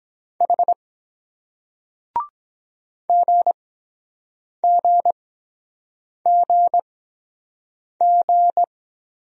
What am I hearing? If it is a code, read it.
Morse code, heard as a keyed tone, about 23 wpm, spaced as for 7 wpm, 700 Hz. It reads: H GGGG